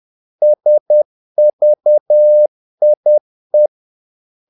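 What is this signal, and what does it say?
Morse code, heard as a keyed tone, about 10 wpm, 600 Hz. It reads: SVIE
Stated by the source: Morse practice of short Danish words at 10 wpm, from CW med Gnister.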